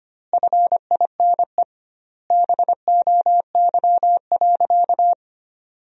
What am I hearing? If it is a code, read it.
Morse code, heard as a keyed tone, about 25 wpm, 700 Hz. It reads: FINE BOY.
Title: 2x Word-Spacing - Sentences from Top 500 Words - 25wpm